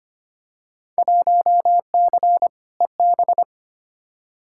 Morse code, heard as a keyed tone, about 25 wpm, 700 Hz. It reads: 1C EB
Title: Field Day Exchanges Encore - 25wpm